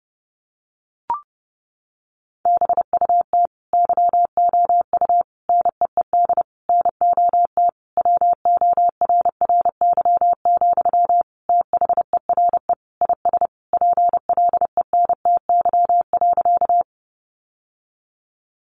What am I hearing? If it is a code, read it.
Morse code, heard as a keyed tone, about 30 wpm, 700 Hz. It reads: BUT YOU NEED NOT WORRY, THERE IS PLENTY.
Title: Sentences from Top 1300 Words 30wpm